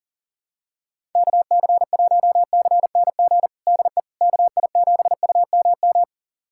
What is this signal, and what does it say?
Morse code, heard as a keyed tone, about 37 wpm, 700 Hz. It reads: KC1CNG DE KI7UMM
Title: USA Callsign DE Callsign 40wpm